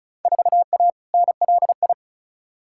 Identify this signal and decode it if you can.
Morse code, heard as a keyed tone, about 35 wpm, 700 Hz. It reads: 4A NLI
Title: Field Day Exchanges Encore - 35wpm